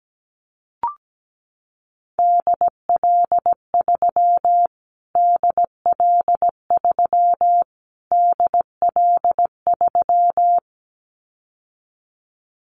Morse code, heard as a keyed tone, about 17 wpm, 700 Hz. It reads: DL3 DL3 DL3
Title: Callsign Prefixes for Top 25 DXCC - Repeated 3x - 17wpm